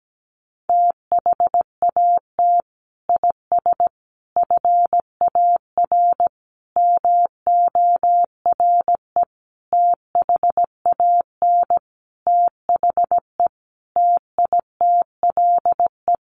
Morse code, heard as a keyed tone, about 17 wpm, 700 Hz. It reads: THAT IS FAR MORE THAN THE TITLE
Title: Sentences from Top 1200 Words 17wpm